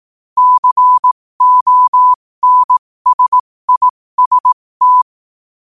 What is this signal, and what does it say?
Morse code, heard as a keyed tone, about 18 wpm, spaced as for 15 wpm, 1000 Hz. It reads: CONSIST
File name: qotdfiles / 15wpm